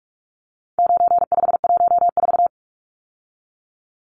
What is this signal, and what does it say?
Morse code, heard as a keyed tone, about 44 wpm, 700 Hz. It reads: 9514
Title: Numbers - 4-Digit 45wpm